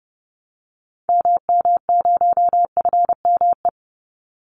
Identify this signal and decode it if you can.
Morse code, heard as a keyed tone, about 29 wpm, 700 Hz. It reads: MM0FME
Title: Call Signs - 30wpm